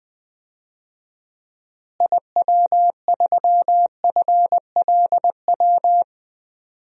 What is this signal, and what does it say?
Morse code, heard as a keyed tone, about 20 wpm, 700 Hz. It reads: IW3FLW
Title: DXCC-100 Call Signs - 6 Characters - Repeated 3x - 20wpm